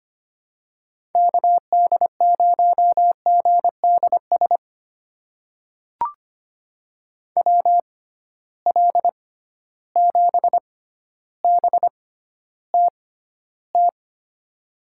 Morse code, heard as a keyed tone, about 25 wpm, spaced as for 13 wpm, 700 Hz. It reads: KD0GDS WL7BTT
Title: USA Call Signs - 6 Characters - 6x Character Spacing - 25wpm